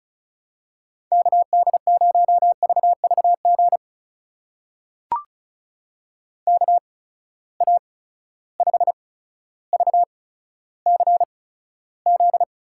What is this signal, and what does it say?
Morse code, heard as a keyed tone, about 36 wpm, spaced as for 15 wpm, 700 Hz. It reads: KD0VVG KA5VCZ
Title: USA Call Signs - 6 Characters - 8x Character Spacing - 35wpm